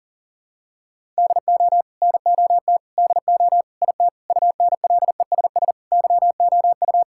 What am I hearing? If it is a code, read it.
Morse code, heard as a keyed tone, about 37 wpm, 700 Hz. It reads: DO NOT DO IT UNLESS YOU
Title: Sentences from Top 1000 Words - Review Part 6 - 40wpm